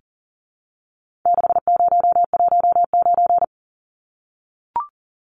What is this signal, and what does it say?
Morse code, heard as a keyed tone, about 40 wpm, 700 Hz. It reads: 6019
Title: Numbers - 4-Digit 40wpm